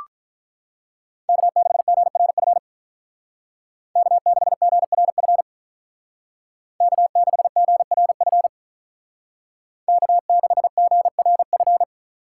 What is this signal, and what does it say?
Morse code, heard as a keyed tone, about 43 wpm, 700 Hz. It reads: K6GRF K6GRF K6GRF K6GRF
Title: Speed Racing - USA Call Signs 35wpm